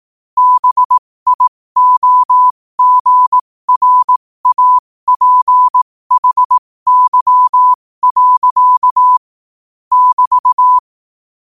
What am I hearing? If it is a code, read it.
Morse code, heard as a keyed tone, about 18 wpm, spaced as for 15 wpm, 1000 Hz. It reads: BIOGRAPHY. <BT>
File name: qotdfiles / 15wpm